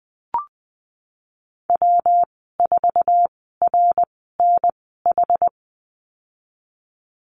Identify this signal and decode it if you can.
Morse code, heard as a keyed tone, about 19 wpm, spaced as for 14 wpm, 700 Hz. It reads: W4RNH
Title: 122 Lesson 20g - Number 3 - Call Signs 20wpm